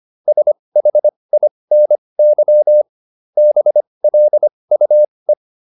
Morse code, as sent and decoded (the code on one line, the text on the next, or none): ... .... .. -. -.-- / -... .-.. ..- .
SHINY BLUE